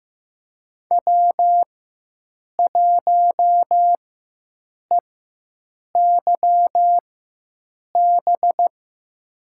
Morse code, heard as a keyed tone, about 15 wpm, spaced as for 7 wpm, 700 Hz. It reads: W1EYB
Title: USA Call Signs - 4x Character Spacing - 15wpm